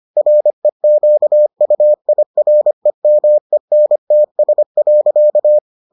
Morse code, heard as a keyed tone, about 25 wpm, 600 Hz. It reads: REQUIREMENTS.